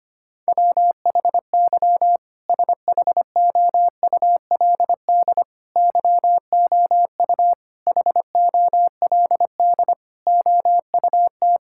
Morse code, heard as a keyed tone, about 25 wpm, 700 Hz. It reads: WHY SHOULD YOU HOLD OUT
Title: Sentences from Top 1600 Words 25wpm